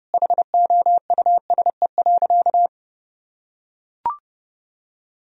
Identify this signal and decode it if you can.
Morse code, heard as a keyed tone, about 30 wpm, 700 Hz. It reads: HOUSE.
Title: Sentences from Top 1000 Words - Review Part 1 - 30wpm